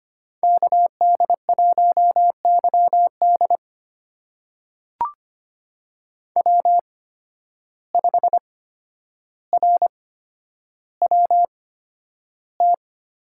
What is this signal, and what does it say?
Morse code, heard as a keyed tone, about 25 wpm, spaced as for 11 wpm, 700 Hz. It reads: KD1YD W5RWT
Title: USA Call Signs - 5 Characters - 8x Character Spacing - 25wpm